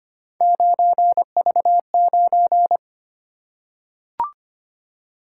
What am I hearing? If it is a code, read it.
Morse code, heard as a keyed tone, about 26 wpm, 700 Hz. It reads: 9V9
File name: Three Letter-Number Combinations - 25wpm